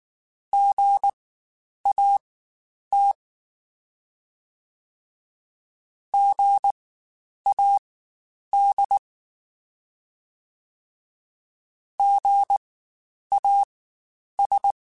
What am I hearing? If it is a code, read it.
Morse code, heard as a keyed tone, about 19 wpm, spaced as for 9 wpm, 800 Hz. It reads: GAT GAD GAS
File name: Spaced Alphas / Spaced Alphas mp3s / Gs sp @18